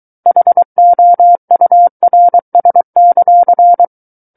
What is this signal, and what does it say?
Morse code, heard as a keyed tone, about 23 wpm, 700 Hz. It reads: HOURS;
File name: Book One / 14-Chapter 14 - 23 WPM